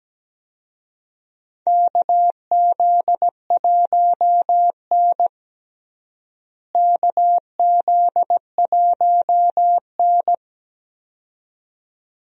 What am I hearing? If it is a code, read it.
Morse code, heard as a keyed tone, about 17 wpm, 700 Hz. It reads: KZ1N KZ1N
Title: USA Call Signs - 4 Characters - Repeated 2x - 17wpm